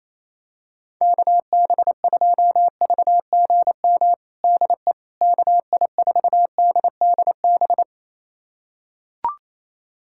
Morse code, heard as a keyed tone, about 28 wpm, 700 Hz. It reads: KB2VGM DE KI4DDB
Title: USA Callsign DE Callsign - 6 Characters - 28wpm